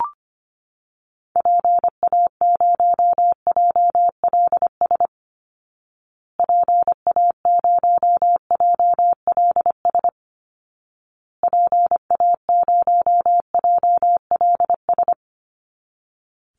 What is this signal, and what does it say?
Morse code, heard as a keyed tone, about 25 wpm, 700 Hz. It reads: PA0JLS PA0JLS PA0JLS
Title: Call Signs Repeated 2x - 25wpm